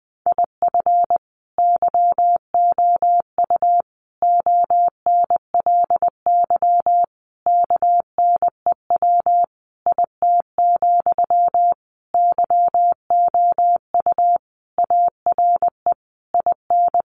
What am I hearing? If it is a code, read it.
Morse code, heard as a keyed tone, about 20 wpm, 700 Hz. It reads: IF YOU ONLY KNEW IT, YOU ARE IN
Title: Sentences from Top 1500 Words 20wpm